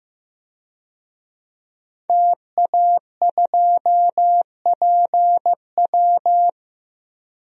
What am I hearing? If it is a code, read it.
Morse code, heard as a keyed tone, about 15 wpm, 700 Hz. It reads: TA2PW
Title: DXCC-100 Call Signs - 5 Characters - Repeated 2x - 15wpm